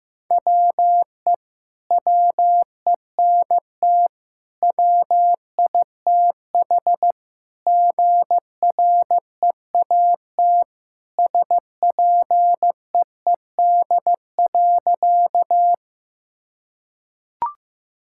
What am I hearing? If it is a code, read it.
Morse code, heard as a keyed tone, about 16 wpm, 700 Hz. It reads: WE WENT WITH GREAT SPEED.
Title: Sentences from Top 1000 Words - Review Part 5 - 15wpm